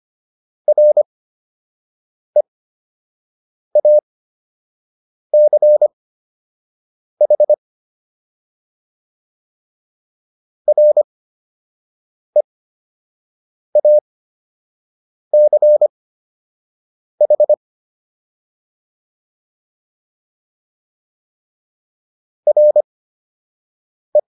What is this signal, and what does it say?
Morse code, heard as a keyed tone, about 25 wpm, spaced as for 6 wpm, 600 Hz. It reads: REACH REACH RE